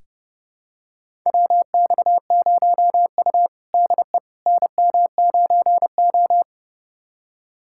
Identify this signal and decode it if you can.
Morse code, heard as a keyed tone, about 31 wpm, 700 Hz. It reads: WX0U DE NM9O
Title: USA Callsign DE Callsign - 4 Characters - 30wpm